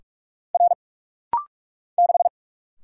Morse code, heard as a keyed tone, about 44 wpm, 690 Hz. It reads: R 6
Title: Single Letter-Number - Rapid-Fire 45wpm